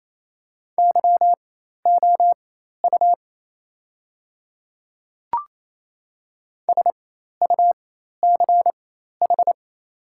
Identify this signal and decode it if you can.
Morse code, heard as a keyed tone, about 28 wpm, spaced as for 13 wpm, 700 Hz. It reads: YOU SUCH
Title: 4x Character Spacing - Top 300 Words - 28wpm